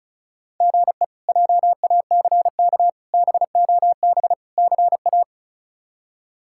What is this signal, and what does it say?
Morse code, heard as a keyed tone, about 36 wpm, 700 Hz. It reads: GE JACK BOB CA
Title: K1USN SST QSO Practice - 35wpm